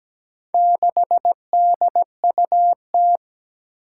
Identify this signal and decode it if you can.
Morse code, heard as a keyed tone, about 17 wpm, 700 Hz. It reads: 6DUT
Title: Four Challenging Letter-Number Combinations 17wpm